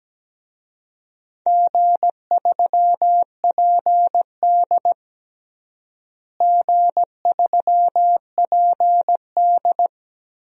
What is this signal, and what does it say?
Morse code, heard as a keyed tone, about 17 wpm, 700 Hz. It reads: G3PD G3PD
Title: DXCC-100 Call Signs - 4 Characters - Repeated 2x - 17wpm